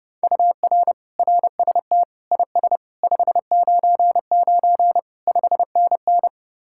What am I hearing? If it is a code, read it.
Morse code, heard as a keyed tone, about 30 wpm, 700 Hz. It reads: UR RST IS 599 5NN